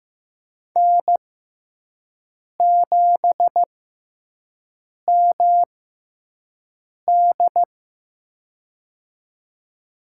Morse code, heard as a keyed tone, about 15 wpm, spaced as for 5 wpm, 700 Hz. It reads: N7MD
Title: USA Call Signs - 6x Character Spacing - 15wpm